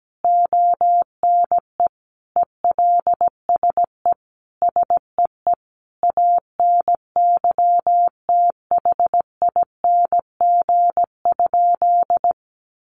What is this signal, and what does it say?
Morse code, heard as a keyed tone, about 17 wpm, 700 Hz. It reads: ONE ELSE SEE ANYTHING?
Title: Sentences from Top 1000 Words 17wpm